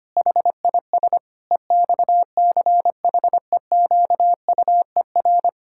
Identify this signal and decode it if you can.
Morse code, heard as a keyed tone, about 25 wpm, 700 Hz. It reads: HIS EXCHEQUER
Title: Pangram Challenge v2 - 25wpm